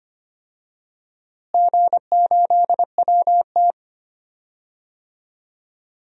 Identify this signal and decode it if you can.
Morse code, heard as a keyed tone, about 25 wpm, 700 Hz. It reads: G8WT
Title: DXCC-100 Call Signs - 4 Characters - Repeated 2x - 25wpm